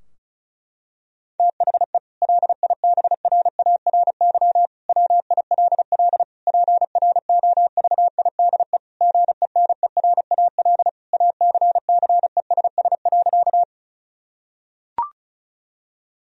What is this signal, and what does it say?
Morse code, heard as a keyed tone, about 34 wpm, 700 Hz. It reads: THE LIBRARY WILL PROVIDE GENERAL ACCESS.
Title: Sentences from Top 1000 Words - Review Part 4 - 35wpm